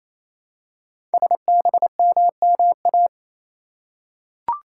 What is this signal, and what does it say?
Morse code, heard as a keyed tone, about 28 wpm, 700 Hz. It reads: SBMMA
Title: Code Groups - 5 Letters - 28wpm